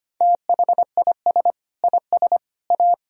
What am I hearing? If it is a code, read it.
Morse code, heard as a keyed tone, about 25 wpm, 700 Hz. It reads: THIS IS A
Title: Sentences from Top 1000 Words - Review Part 4 - 25wpm